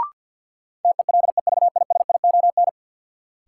Morse code, heard as a keyed tone, about 48 wpm, 700 Hz. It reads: TELEVISION
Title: Sentences from Top 1000 Words - Review Part 4 - New Word Drill - 50wpm